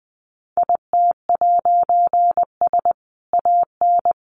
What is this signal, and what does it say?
Morse code, heard as a keyed tone, about 20 wpm, 700 Hz. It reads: IT'S AN
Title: A Sound Of Thunder 20wpm